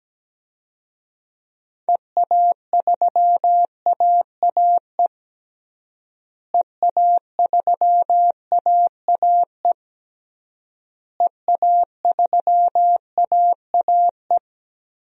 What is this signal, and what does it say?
Morse code, heard as a keyed tone, about 17 wpm, 700 Hz. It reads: EA3AAE EA3AAE EA3AAE
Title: DXCC-100 Call Signs - 6 Characters - Repeated 3x - 17wpm